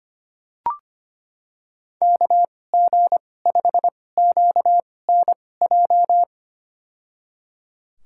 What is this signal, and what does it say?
Morse code, heard as a keyed tone, about 25 wpm, spaced as for 18 wpm, 700 Hz. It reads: KG5QNJ
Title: USA Call Signs - 6 Characters - 2x Character Spacing - 25wpm